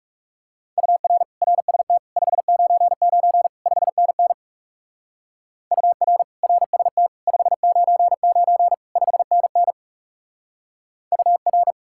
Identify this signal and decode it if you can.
Morse code, heard as a keyed tone, about 36 wpm, 700 Hz. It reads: UR RST 599 5NN UR RST 599 5NN UR
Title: Speed-Racing Common QSO Phrases - 30wpm